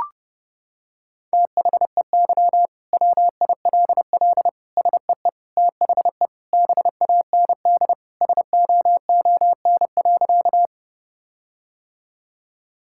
Morse code, heard as a keyed tone, about 32 wpm, 700 Hz. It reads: THEY WILL SEE THE BAND SOON.
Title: Sentences from Top 1600 Words 30wpm